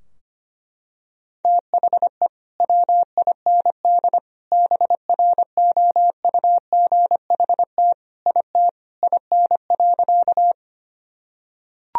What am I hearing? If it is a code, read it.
Morse code, heard as a keyed tone, about 25 wpm, 700 Hz. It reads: THE WIND BROUGHT IT IN.
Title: Sentences from Top 1000 Words - Review Part 2 - 25wpm